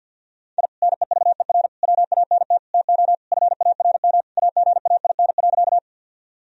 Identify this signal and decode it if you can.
Morse code, heard as a keyed tone, about 50 wpm, 700 Hz. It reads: I NEVER WANT TO FARM AGAIN.
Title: Sentences from Top 1000 Words - Review Part 2 - 50wpm